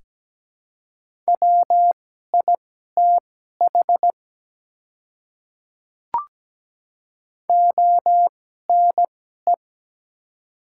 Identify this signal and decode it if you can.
Morse code, heard as a keyed tone, about 17 wpm, spaced as for 12 wpm, 700 Hz. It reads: WITH ONE
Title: 2x Character Spacing - Top 300 Words - 17wpm